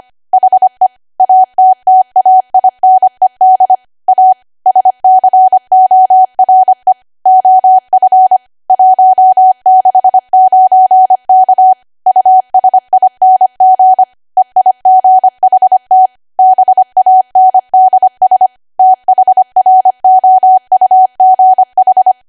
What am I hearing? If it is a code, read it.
Morse code, heard as a keyed tone, about 25 wpm, 750 Hz. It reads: HE ATTAINED A SCORE OF 169K USING EIGHT BANDS THROUGH